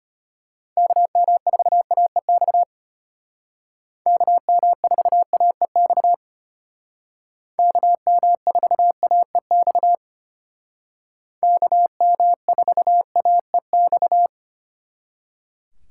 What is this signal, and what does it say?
Morse code, heard as a keyed tone, about 30 wpm, 700 Hz. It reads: KM4AEX KM4AEX KM4AEX KM4AEX
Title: Speed Racing - USA 2x3 Call Signs 25wpm